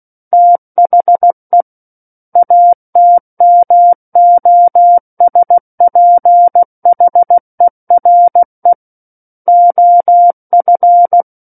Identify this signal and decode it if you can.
Morse code, heard as a keyed tone, about 16 wpm, 700 Hz. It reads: THE ATMOSPHERE OF
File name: Book One / 07-Chapter 7 - 16 WPM